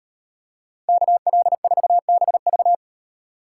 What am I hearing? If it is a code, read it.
Morse code, heard as a keyed tone, about 38 wpm, 700 Hz. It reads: KL4BV